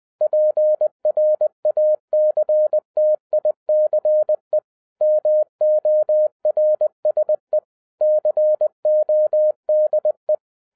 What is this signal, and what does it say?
Morse code, heard as a keyed tone, about 20 wpm, 610 Hz. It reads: PRACTICE MORSE CODE
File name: 2011-10-mp3 / n13-TIPS FROM MY 34 YEARS OF EXPERIENCE 20wpm